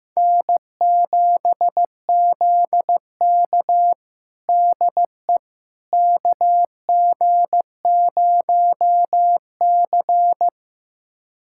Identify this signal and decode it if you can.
Morse code, heard as a keyed tone, about 15 wpm, 700 Hz. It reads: N7ZK DE KG0C
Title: USA Callsign DE Callsign - 4 Characters - 15wpm